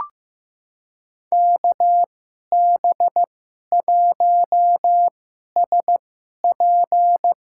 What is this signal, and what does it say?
Morse code, heard as a keyed tone, about 15 wpm, spaced as for 11 wpm, 700 Hz. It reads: KB1SP
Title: USA Call Signs - 2x Character Spacing - 15wpm